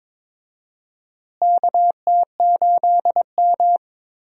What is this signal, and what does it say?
Morse code, heard as a keyed tone, about 22 wpm, 700 Hz. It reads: KT8M
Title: USA Call Signs - 4 Characters - Repeated 3x - 22wpm